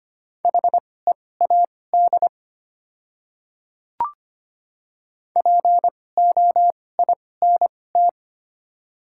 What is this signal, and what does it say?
Morse code, heard as a keyed tone, about 25 wpm, spaced as for 18 wpm, 690 Hz. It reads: HEAD POINT
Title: 2x Character Spacing - Top 300 Words - 25wpm